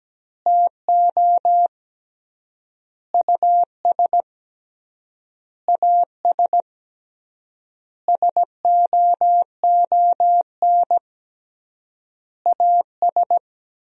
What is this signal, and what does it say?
Morse code, heard as a keyed tone, about 17 wpm, 700 Hz. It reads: TO US AS SOON AS
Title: 3x Word-Spacing - Sentences from Top 500 Words - 17wpm